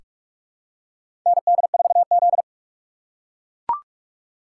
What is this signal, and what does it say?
Morse code, heard as a keyed tone, about 44 wpm, 700 Hz. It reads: ND4Z
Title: USA Call Signs - 4 Characters - 2x Character Spacing - 45wpm